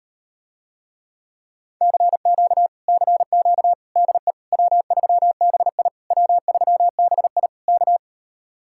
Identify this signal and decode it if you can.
Morse code, heard as a keyed tone, about 39 wpm, 700 Hz. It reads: CQ CQ DE W3BI W3BI K